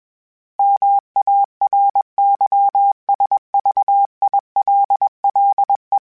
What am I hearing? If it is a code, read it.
Morse code, heard as a keyed tone, about 21 wpm, 800 Hz. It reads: MARYSVILLE
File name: fromSMs / FromSMs mp3s / Cities@20